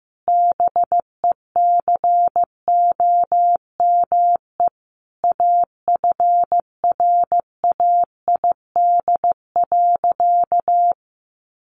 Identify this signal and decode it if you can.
Morse code, heard as a keyed tone, about 15 wpm, 700 Hz. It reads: BECOME AFRAID.
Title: Sentences from Top 1500 Words 15wpm